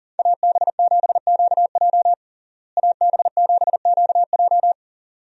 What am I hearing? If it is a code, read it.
Morse code, heard as a keyed tone, about 40 wpm, 700 Hz. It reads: AB7QJ AB7QJ
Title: USA Call Signs - 5 Characters - Repeated 2x - 40wpm